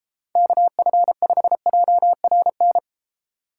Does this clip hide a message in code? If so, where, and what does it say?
Yes: Morse code from 0.4 to 2.8 s, heard as a keyed tone, about 30 wpm, 700 Hz. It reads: KF5JRN